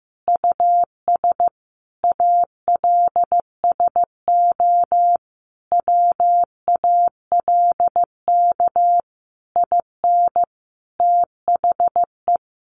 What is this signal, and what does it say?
Morse code, heard as a keyed tone, about 15 wpm, 700 Hz. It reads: US ALSO WALK IN THE
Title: Sentences from Top 1400 Words 15wpm